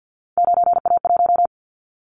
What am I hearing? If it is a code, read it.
Morse code, heard as a keyed tone, about 50 wpm, 700 Hz. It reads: 9A1
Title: Callsign Prefixes for Top 25 DXCC - Repeated 3x - 50wpm